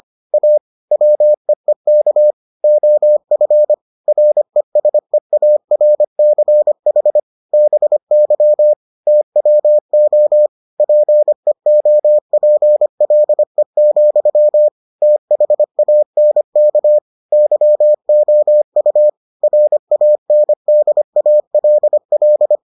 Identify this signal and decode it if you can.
Morse code, heard as a keyed tone, about 25 wpm, 600 Hz. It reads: A WEEK OF RESEARCH BY TWO PEOPLE, THANK YOU RANDALL